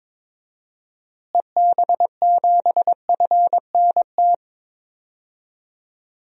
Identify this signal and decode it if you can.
Morse code, heard as a keyed tone, about 22 wpm, 700 Hz. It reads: EB7FNT
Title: DXCC-100 Call Signs - 6 Characters - Repeated 2x - 22wpm